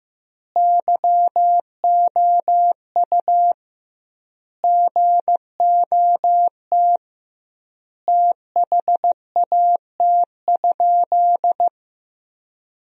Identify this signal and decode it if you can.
Morse code, heard as a keyed tone, about 15 wpm, 700 Hz. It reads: YOU GOT THAT?
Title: 2x Word-Spacing - Sentences from Top 300 Words - 15wpm